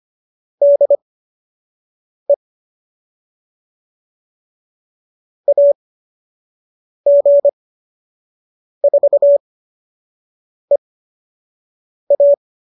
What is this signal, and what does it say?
Morse code, heard as a keyed tone, about 25 wpm, spaced as for 6 wpm, 570 Hz. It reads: DE AG4EA